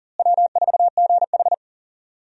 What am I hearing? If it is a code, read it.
Morse code, heard as a keyed tone, about 40 wpm, 700 Hz. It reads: W4GH
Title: USA Call Signs - 4 Characters - Repeated 3x - 40wpm